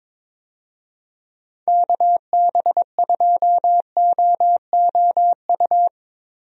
Morse code, heard as a keyed tone, about 22 wpm, 700 Hz. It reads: KB2OOU
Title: USA Call Signs - 6 Characters - Repeated 2x - 22wpm